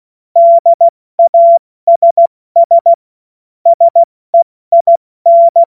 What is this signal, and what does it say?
Morse code, heard as a keyed tone, about 16 wpm, 680 Hz. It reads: DASS SEIN
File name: Morsekurs / Witze / Witz 30. August 2022 70 F80